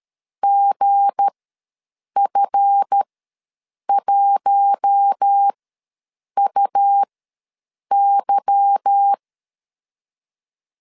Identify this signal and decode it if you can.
Morse code, heard as a keyed tone, about 13 wpm, spaced as for 7 wpm, 800 Hz. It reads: GF1UY
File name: LEZIONI / 8 YJ45 / 8 YJ45 ws 12wpm 3146